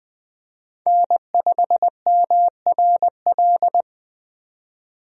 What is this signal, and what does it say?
Morse code, heard as a keyed tone, about 20 wpm, 700 Hz. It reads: N5MRL
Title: USA Call Signs - 5 Characters - 20wpm